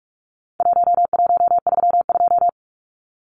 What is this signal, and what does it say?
Morse code, heard as a keyed tone, about 44 wpm, 700 Hz. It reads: 1132
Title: Numbers - Contest Style - 4-Digit 45wpm